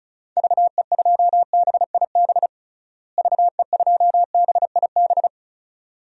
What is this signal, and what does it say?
Morse code, heard as a keyed tone, about 34 wpm, 700 Hz. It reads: VE2BIB VE2BIB
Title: DXCC-100 Call Signs - 6 Characters - Repeated 2x - 35wpm